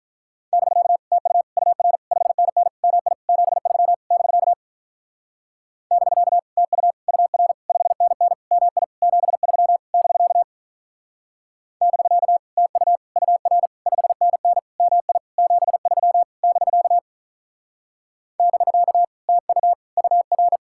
Morse code, heard as a keyed tone, about 42 wpm, 700 Hz. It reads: <BK> TU UR 5NN MI 73 <BK> <BK> TU UR 5NN MI 73 <BK> <BK> TU UR 5NN MI 73 <BK> <BK> TU UR
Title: Speed-Racing Common QSO Phrases - 35wpm